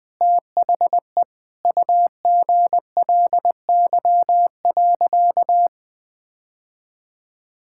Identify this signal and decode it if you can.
Morse code, heard as a keyed tone, about 20 wpm, 700 Hz. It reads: THE UGLY.
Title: Sentences with 6-Word Phrases - Spoken Beginning - 20wpm